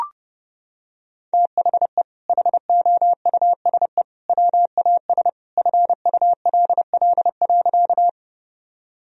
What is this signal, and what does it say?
Morse code, heard as a keyed tone, about 30 wpm, 700 Hz. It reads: THE HOUSE WAS FULL.
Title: Sentences from Top 1000 Words - Review Part 3 - 30wpm